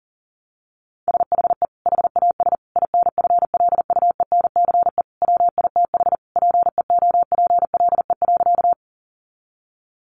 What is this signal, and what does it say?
Morse code, heard as a keyed tone, about 40 wpm, 700 Hz. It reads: SHE HAS INFLUENCE WITH PEOPLE.
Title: Sentences from Top 1100 Words 40wpm